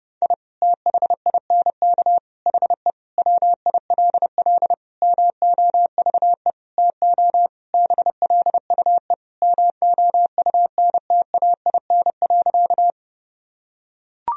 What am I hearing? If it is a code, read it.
Morse code, heard as a keyed tone, about 30 wpm, 700 Hz. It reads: I THINK HE WILL MOVE TO BLUE MOUNTAIN.
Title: Sentences from Top 1000 Words - Review Part 3 - 30wpm